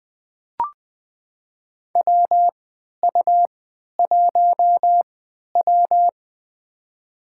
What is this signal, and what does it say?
Morse code, heard as a keyed tone, about 21 wpm, spaced as for 11 wpm, 700 Hz. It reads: WU1W
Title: USA Call Signs - 4 Characters - 3x Character Spacing - 20wpm